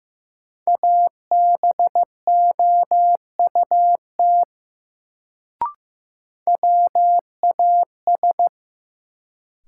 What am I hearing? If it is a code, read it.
Morse code, heard as a keyed tone, about 15 wpm, 700 Hz. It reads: ABOUT WAS
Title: Sets of 2 Words - New Word Drill - 15wpm